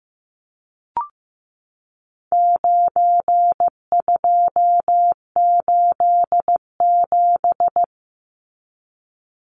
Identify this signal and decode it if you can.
Morse code, heard as a keyed tone, about 15 wpm, 700 Hz. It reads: 9287
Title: Numbers - 4-Digit 15wpm